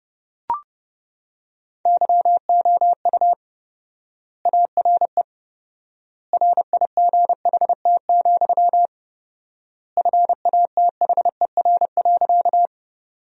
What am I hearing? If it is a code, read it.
Morse code, heard as a keyed tone, about 31 wpm, 700 Hz. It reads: YOU ARE RIGHT, FATHER.
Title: 4x Word-Spacing - Sentences from Top 300 Words - 30wpm